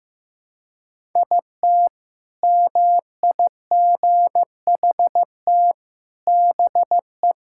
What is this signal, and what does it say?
Morse code, heard as a keyed tone, about 15 wpm, 700 Hz. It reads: IT MIGHT BE
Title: Sentences from Top 1000 Words - Review Part 6 - 15wpm